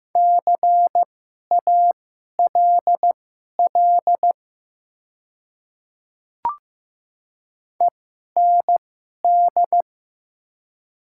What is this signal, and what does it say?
Morse code, heard as a keyed tone, about 15 wpm, spaced as for 11 wpm, 700 Hz. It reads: CALL END
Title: 2x Character Spacing - Top 200 Words - 15wpm